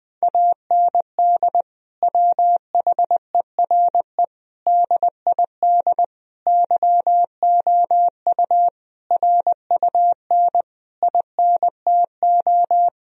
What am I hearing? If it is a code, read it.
Morse code, heard as a keyed tone, about 21 wpm, 700 Hz. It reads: AND WHERE DID YOU RUN INTO
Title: Sentences from Top 1000 Words - Review Part 1 - 20wpm